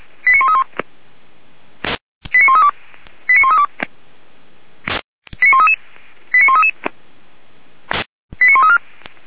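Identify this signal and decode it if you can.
Five-tone selective call (ZVEI), heard as quick runs of five tones, tones of about 80 ms. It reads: 89131 89132 89132 89133 89133 89134
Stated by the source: Five-tone ZVEI selective call received with an SDR.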